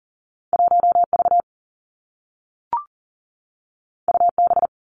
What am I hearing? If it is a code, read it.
Morse code, heard as a keyed tone, about 40 wpm, 700 Hz. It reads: 1 V U B
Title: Two Challenging Letter-Number Combinations 40wpm